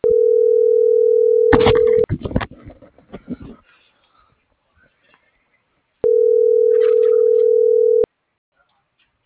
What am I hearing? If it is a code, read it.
Ringback tone (North America), 440 + 480 Hz.